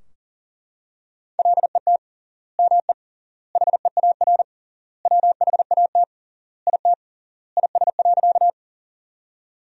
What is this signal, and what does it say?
Morse code, heard as a keyed tone, about 40 wpm, 700 Hz. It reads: LET ME HEAR WHAT IT IS.
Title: 3x Word-Spacing - Sentences from Top 300 Words - 40wpm